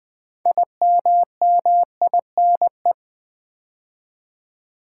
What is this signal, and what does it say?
Morse code, heard as a keyed tone, about 20 wpm, 700 Hz. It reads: IMMINE
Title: Build A Word - 3 Letters At A Time - 8-Letter Words - 20wpm